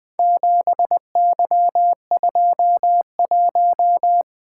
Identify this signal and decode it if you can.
Morse code, heard as a keyed tone, about 20 wpm, 700 Hz. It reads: 7Y21
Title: Four Letter-Number Combinations - 20wpm